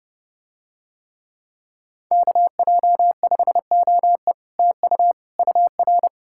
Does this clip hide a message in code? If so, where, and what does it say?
Yes: Morse code from 2.1 to 6.1 s, heard as a keyed tone, about 29 wpm, 700 Hz. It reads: KJ5OE TU UR